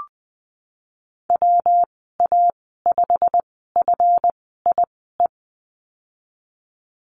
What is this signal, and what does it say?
Morse code, heard as a keyed tone, about 20 wpm, spaced as for 15 wpm, 700 Hz. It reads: WA5FIE
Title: 167 Lesson 27g - Number 7 - Call Signs 20wpm